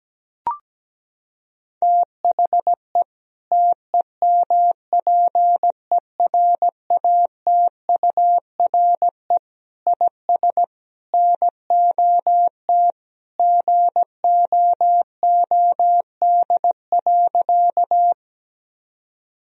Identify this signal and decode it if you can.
Morse code, heard as a keyed tone, about 17 wpm, 700 Hz. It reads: THE TEMPERATURE IS NOT GOOD.
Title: Sentences from Top 1000 Words - Review Part 4 - 17wpm